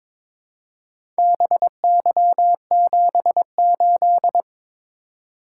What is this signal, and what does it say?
Morse code, heard as a keyed tone, about 21 wpm, 700 Hz. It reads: BY78
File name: practice / 22wpm / Four Letter-Number Combinations - 22wpm